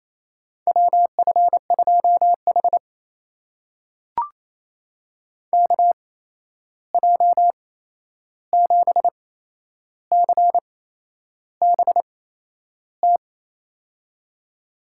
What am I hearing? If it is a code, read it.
Morse code, heard as a keyed tone, about 28 wpm, spaced as for 11 wpm, 700 Hz. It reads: WF2H KJ7CBT
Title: USA Call Signs - 8x Character Spacing - 28wpm